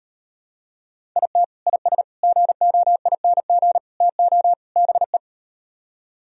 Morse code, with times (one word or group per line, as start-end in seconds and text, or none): 1.2-1.4 IT
1.7-2.0 IS
2.2-3.8 GOING
4.0-4.5 TO
4.8-5.2 BE